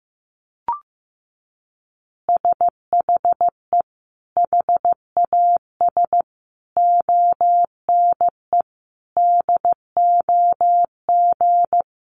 Morse code, heard as a keyed tone, about 15 wpm, 700 Hz. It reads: SHE HAS ONE DOG
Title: Easy Phrases 15wpm